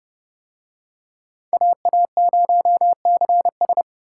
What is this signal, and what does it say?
Morse code, heard as a keyed tone, about 29 wpm, 700 Hz. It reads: AA0CS